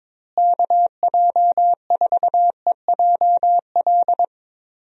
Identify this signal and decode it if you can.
Morse code, heard as a keyed tone, about 22 wpm, 700 Hz. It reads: KJ4EJL
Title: USA Call Signs - 6 Characters - Repeated 2x - 22wpm